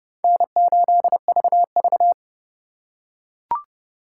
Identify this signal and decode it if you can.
Morse code, heard as a keyed tone, about 26 wpm, 700 Hz. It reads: N8VV